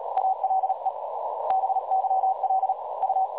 Morse code, heard as a keyed tone, about 36 wpm, 760 Hz, with noise. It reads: ALE BYL U
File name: dlazaawansowanych / nagraniazpasma / websdr recording 3565.0kHz